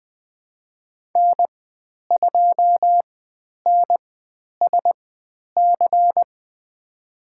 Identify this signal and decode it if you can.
Morse code, heard as a keyed tone, about 20 wpm, spaced as for 10 wpm, 700 Hz. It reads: N2NSC
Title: 102 Lesson 17g - Letter u - Call Signs 20wpm